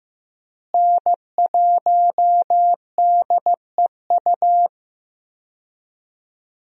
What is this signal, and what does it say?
Morse code, heard as a keyed tone, about 15 wpm, 700 Hz. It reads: N1DEU
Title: USA Call Signs - 5 Characters - Repeated 2x - 15wpm